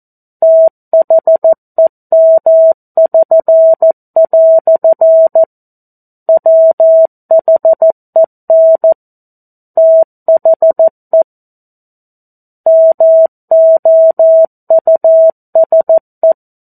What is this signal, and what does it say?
Morse code, heard as a keyed tone, about 14 wpm, 650 Hz. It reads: THEM<SN>" WHEN THE MOUSE
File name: 02-Chapter-02 14 WPM